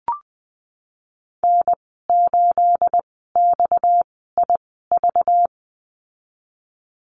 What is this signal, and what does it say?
Morse code, heard as a keyed tone, about 19 wpm, spaced as for 14 wpm, 700 Hz. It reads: N8XIV
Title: 231 Lesson 37g - Letter x - Call Signs 20wpm